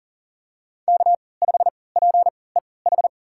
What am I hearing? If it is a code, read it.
Morse code, heard as a keyed tone, about 40 wpm, spaced as for 23 wpm, 690 Hz. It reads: K5PEH
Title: USA Call Signs - 5 Characters - 3x Character Spacing - 40wpm